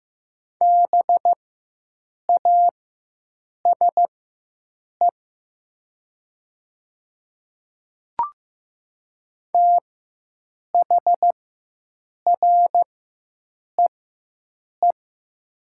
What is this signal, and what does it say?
Morse code, heard as a keyed tone, about 15 wpm, spaced as for 7 wpm, 700 Hz. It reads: BASE THREE
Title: 4x Character Spacing - Top 300 Words - 15wpm